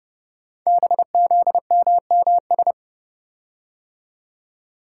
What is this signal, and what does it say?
Morse code, heard as a keyed tone, about 28 wpm, 700 Hz. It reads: BZMMS
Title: Code Groups - 5 Letters - 30wpm